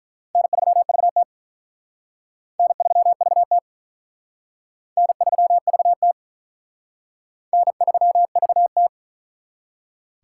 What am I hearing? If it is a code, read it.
Morse code, heard as a keyed tone, about 44 wpm, 700 Hz. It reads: N3VT N3VT N3VT N3VT